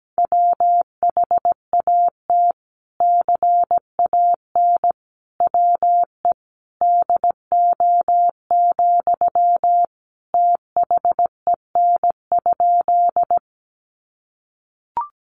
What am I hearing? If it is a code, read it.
Morse code, heard as a keyed tone, about 17 wpm, 700 Hz. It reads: WHAT CAN WE DO, THEN?
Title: Sentences from Top 100 Words 17wpm